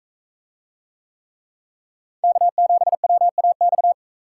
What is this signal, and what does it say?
Morse code, heard as a keyed tone, about 43 wpm, 700 Hz. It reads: K7WAX